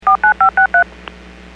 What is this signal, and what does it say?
DTMF, heard as telephone touch tones, tones of about 110 ms. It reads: 49563
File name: WPS / 00396 22-46 WPS 451.25 WPS